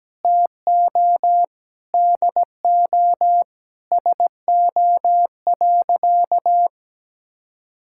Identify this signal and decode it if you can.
Morse code, heard as a keyed tone, about 17 wpm, 700 Hz. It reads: TO DO SO.
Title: Sentences from Top 1000 Words - Review Part 6 - 17wpm